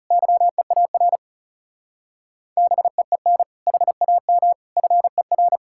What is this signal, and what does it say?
Morse code, heard as a keyed tone, about 34 wpm, 700 Hz. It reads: YEAR BEEN HAM FER
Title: Speed-Racing Common QSO Phrases Encore - 30wpm